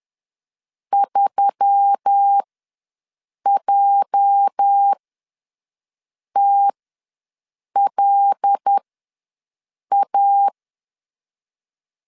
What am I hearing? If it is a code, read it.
Morse code, heard as a keyed tone, about 10 wpm, spaced as for 5 wpm, 800 Hz. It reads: 3JTLA
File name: LEZIONI / 9 2867 ws 10wpm 4215